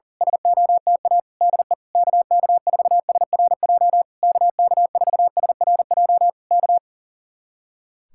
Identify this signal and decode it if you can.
Morse code, heard as a keyed tone, about 36 wpm, 700 Hz. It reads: SOTA DE KK4SRJ KK4SRJ K
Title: Speed-Racing Common QSO Phrases - 40wpm